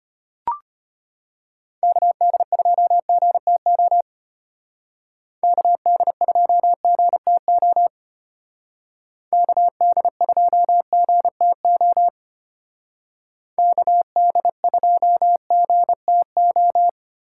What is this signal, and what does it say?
Morse code, heard as a keyed tone, about 31 wpm, 700 Hz. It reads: KD2GTO KD2GTO KD2GTO KD2GTO
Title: Speed Racing - USA 2x3 Call Signs 25wpm